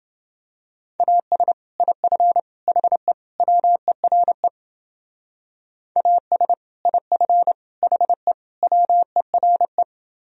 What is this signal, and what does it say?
Morse code, heard as a keyed tone, about 29 wpm, 700 Hz. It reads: AS IF HE WERE AS IF HE WERE